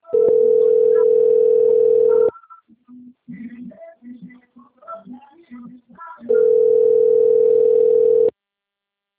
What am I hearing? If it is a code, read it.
Ringback tone (North America), 440 + 480 Hz.